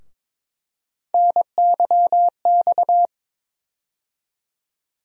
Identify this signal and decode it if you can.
Morse code, heard as a keyed tone, about 21 wpm, 700 Hz. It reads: NYX
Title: USA Call Sign Postfixes - 3 Letters 22wpm